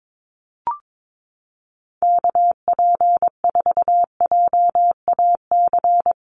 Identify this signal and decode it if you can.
Morse code, heard as a keyed tone, about 22 wpm, 700 Hz. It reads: KP4JAC
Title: USA Call Signs 22wpm